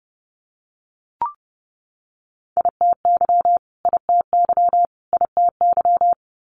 Morse code, heard as a keyed tone, about 31 wpm, 700 Hz. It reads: ITY ITY ITY
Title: Multi-letter Phonemes Repeated 3x 30wpm